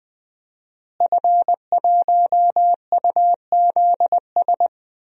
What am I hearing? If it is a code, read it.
Morse code, heard as a keyed tone, about 20 wpm, 700 Hz. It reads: F1UZS
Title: DXCC-100 Call Signs - 5 Characters - Repeated 3x - 20wpm